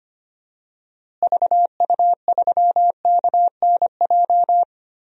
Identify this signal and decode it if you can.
Morse code, heard as a keyed tone, about 25 wpm, 700 Hz. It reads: VU3KNJ
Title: DXCC-100 Call Signs - 6 Characters - Repeated 2x - 25wpm